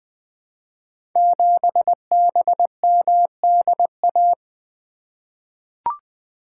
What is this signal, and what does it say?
Morse code, heard as a keyed tone, about 20 wpm, 700 Hz. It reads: 7BMDA
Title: Code Groups - 5 Letter:Numbers - 20wpm